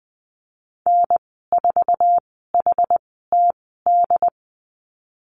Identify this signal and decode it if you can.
Morse code, heard as a keyed tone, about 20 wpm, spaced as for 15 wpm, 700 Hz. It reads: N4HTD
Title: 069 Lesson 12g - Letter d - Call Signs 20wpm